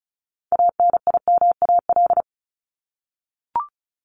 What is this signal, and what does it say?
Morse code, heard as a keyed tone, about 34 wpm, 700 Hz. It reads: ANIMAL
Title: Top 200 Words 35wpm